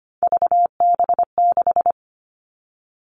Morse code, heard as a keyed tone, about 25 wpm, 700 Hz. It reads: VB6
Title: All Cut Numbers - 3-Digit 25wpm